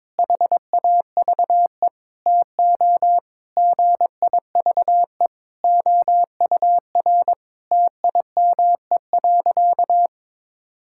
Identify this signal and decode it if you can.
Morse code, heard as a keyed tone, about 22 wpm, 700 Hz. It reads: HAVE TO GIVE OUR TIME.
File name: practice / 22wpm / Sentences with 4-Word Phrases - 22wpm